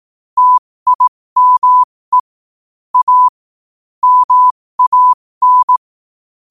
Morse code, heard as a keyed tone, about 18 wpm, spaced as for 15 wpm, 1000 Hz. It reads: TIME A MAN
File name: 15wpm